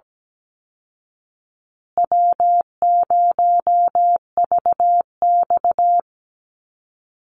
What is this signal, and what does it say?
Morse code, heard as a keyed tone, about 17 wpm, 700 Hz. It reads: W0VX
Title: Call Signs Repeated 2x - 17wpm